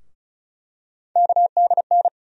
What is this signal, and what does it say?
Morse code, heard as a keyed tone, about 34 wpm, 700 Hz. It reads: KDN
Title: USA Call Sign Postfixes - 3 Letters 35wpm